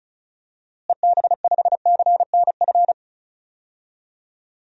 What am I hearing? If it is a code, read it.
Morse code, heard as a keyed tone, about 34 wpm, 700 Hz. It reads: EB5CNF
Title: DXCC-100 Call Signs - 6 Characters - Repeated 2x - 35wpm